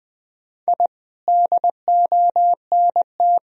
Morse code, heard as a keyed tone, about 21 wpm, 700 Hz. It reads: I DONT
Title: Sentences from Top 1000 Words - Review Part 4 - 20wpm